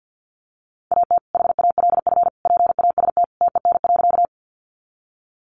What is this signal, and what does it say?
Morse code, heard as a keyed tone, about 49 wpm, 700 Hz. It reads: AT HALF PAST TEN.
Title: Sentences from Top 600 Words 50wpm